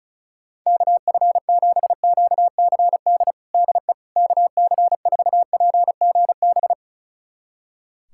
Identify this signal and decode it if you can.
Morse code, heard as a keyed tone, about 36 wpm, 700 Hz. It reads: KF7QCD DE KC4PGB
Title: USA Callsign DE Callsign 35wpm